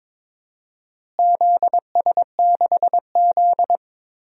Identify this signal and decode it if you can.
Morse code, heard as a keyed tone, about 21 wpm, 700 Hz. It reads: ZS6Z
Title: DXCC-100 Call Signs - 4 Characters - Repeated 2x - 22wpm